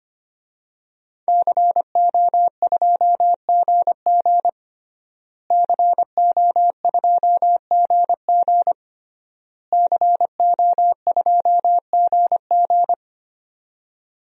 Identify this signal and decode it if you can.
Morse code, heard as a keyed tone, about 25 wpm, 700 Hz. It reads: CO2GG CO2GG CO2GG
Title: DXCC-100 Call Signs - 5 Characters - Repeated 3x - 25wpm